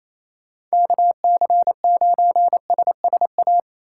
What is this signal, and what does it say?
Morse code, heard as a keyed tone, about 33 wpm, 700 Hz. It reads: KC9SSA
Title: Speed Racing - USA Call Signs 28wpm